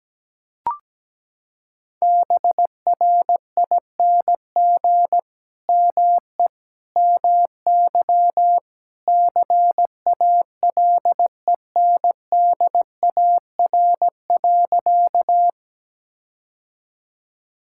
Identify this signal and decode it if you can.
Morse code, heard as a keyed tone, about 17 wpm, 700 Hz. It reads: BRING ME MY CALENDAR.
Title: Sentences from Top 1600 Words 17wpm